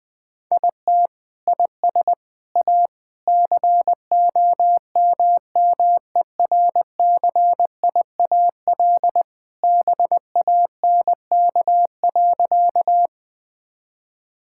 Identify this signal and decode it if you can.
Morse code, heard as a keyed tone, about 20 wpm, 700 Hz. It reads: IT IS A COMMERCIAL BANK.
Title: Sentences from Top 1000 Words - Review Part 6 - 20wpm